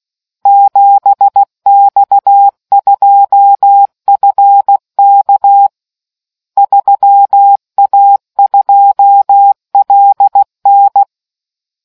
Morse code, heard as a keyed tone, about 16 wpm, 800 Hz. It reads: 7X2FK 3A2LN